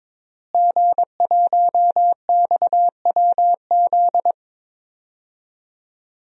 Morse code, heard as a keyed tone, about 22 wpm, 700 Hz. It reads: G1XWZ
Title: DXCC-100 Call Signs - 5 Characters - Repeated 3x - 22wpm